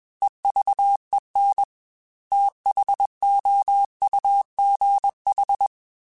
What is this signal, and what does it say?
Morse code, heard as a keyed tone, about 21 wpm, 800 Hz. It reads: EVEN THOUGH
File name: Technical Subjects / Technical Subjects-mp3s / Sw Pwr Sup @20